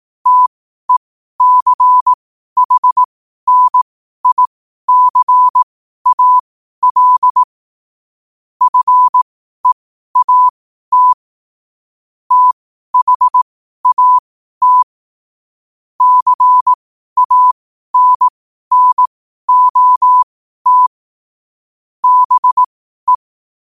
Morse code, heard as a keyed tone, about 18 wpm, spaced as for 12 wpm, 1000 Hz. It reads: TECHNICAL FEAT THAT CANNOT BE
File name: qotdfiles / 12wpm